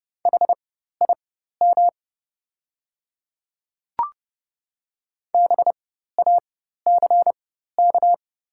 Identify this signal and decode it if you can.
Morse code, heard as a keyed tone, about 30 wpm, spaced as for 14 wpm, 690 Hz. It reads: HIM BACK
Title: 4x Character Spacing - Top 300 Words - 30wpm